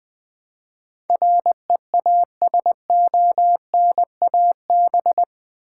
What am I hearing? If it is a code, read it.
Morse code, heard as a keyed tone, about 20 wpm, 700 Hz. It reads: REASONAB
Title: Build A Word - 4 Letters At A Time - 10-Letter Words - 20wpm